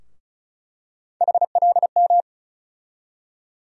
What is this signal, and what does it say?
Morse code, heard as a keyed tone, about 34 wpm, 700 Hz. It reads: HLM
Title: USA Call Sign Postfixes - 3 Letters 35wpm